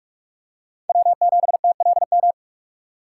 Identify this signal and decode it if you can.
Morse code, heard as a keyed tone, about 46 wpm, 700 Hz. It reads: W7TLM
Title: USA Call Signs - 5 Characters - 2x Character Spacing - 45wpm